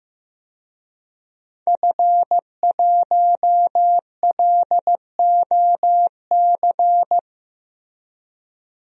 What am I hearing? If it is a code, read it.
Morse code, heard as a keyed tone, about 16 wpm, 700 Hz. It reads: F1LOC
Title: DXCC-100 Call Signs - 5 Characters - Repeated 3x - 15wpm